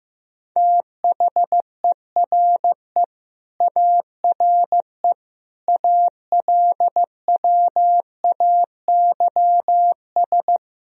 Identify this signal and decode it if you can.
Morse code, heard as a keyed tone, about 15 wpm, 700 Hz. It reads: THERE ARE ALWAYS
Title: Sentences from Top 1000 Words - Review Part 4 - 15wpm